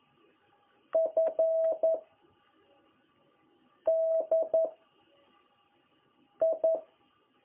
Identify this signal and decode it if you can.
Morse code, heard as a keyed tone, about 11 wpm, 660 Hz. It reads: F D I
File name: jh8cbh / 2021cw